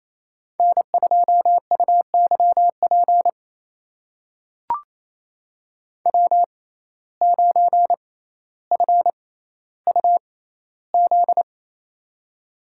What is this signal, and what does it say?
Morse code, heard as a keyed tone, about 28 wpm, spaced as for 14 wpm, 700 Hz. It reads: N2UYP W9FUZ
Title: USA Call Signs - 5 Characters - 6x Character Spacing - 28wpm